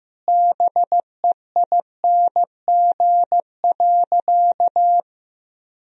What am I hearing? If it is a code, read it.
Morse code, heard as a keyed tone, about 15 wpm, 700 Hz. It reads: BEING.
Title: Sentences from Top 1000 Words - Review Part 4 - 15wpm